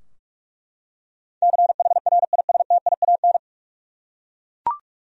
Morse code, heard as a keyed tone, about 42 wpm, 700 Hz. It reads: CHRISTIAN
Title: USA Common Names - 45wpm